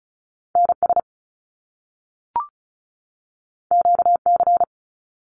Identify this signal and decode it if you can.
Morse code, heard as a keyed tone, about 36 wpm, 700 Hz. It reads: NS QC
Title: Canadian Provinces and Territories 35wpm